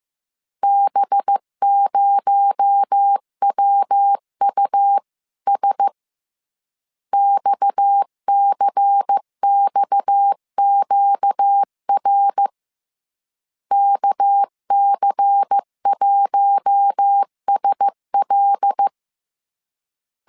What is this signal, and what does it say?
Morse code, heard as a keyed tone, about 15 wpm, 800 Hz. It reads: B0WUS XCXQR KC1SL
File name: CW / DET / 1 6 1500C S B 14wpm 2958